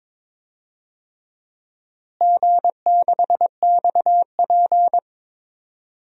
Morse code, heard as a keyed tone, about 22 wpm, 700 Hz. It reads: G6XP
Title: DXCC-100 Call Signs - 4 Characters - Repeated 3x - 22wpm